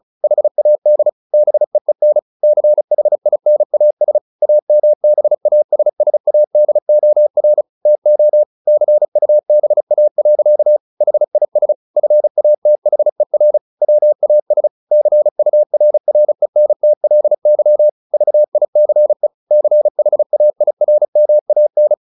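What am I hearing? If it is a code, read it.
Morse code, heard as a keyed tone, about 34 wpm, 600 Hz. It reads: HAD BEEN CHINAS AMBASSADOR TO CUBA. HIS FATHER WAS CURRENTLY VICE CHAIRMAN
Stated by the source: Morse practice at 35 wpm from CW med Gnister, sending Ken Follett's novel Never in English.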